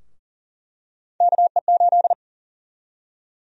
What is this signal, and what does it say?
Morse code, heard as a keyed tone, about 40 wpm, 700 Hz. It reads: KE8
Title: USA Call Sign Prefixes - Repeated 2x - 40wpm